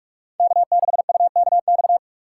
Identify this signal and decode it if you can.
Morse code, heard as a keyed tone, about 42 wpm, 700 Hz. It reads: K6UKX
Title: USA Call Signs - 5 Characters - 3x Character Spacing - 45wpm